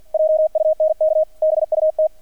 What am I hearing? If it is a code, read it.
Morse code, heard as a keyed tone, about 50 wpm, 630 Hz, with radio noise. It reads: JUNK BUT
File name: archived / CRSnet